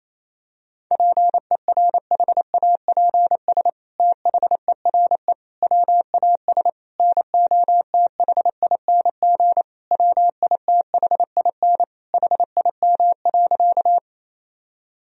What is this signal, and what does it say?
Morse code, heard as a keyed tone, about 28 wpm, 700 Hz. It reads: PERHAPS THERE WAS NOTHING WITHIN HIM.
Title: Sentences from Top 1000 Words - Review Part 4 - 28wpm